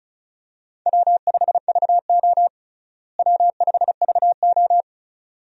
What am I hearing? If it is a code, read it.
Morse code, heard as a keyed tone, about 36 wpm, 700 Hz. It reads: W5VO W5VO
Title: USA Call Signs - 4 Characters - Repeated 2x - 35wpm